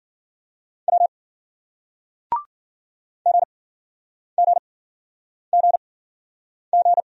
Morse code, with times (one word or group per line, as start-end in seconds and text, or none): 0.9-1.1 W
3.3-7.0 GGGG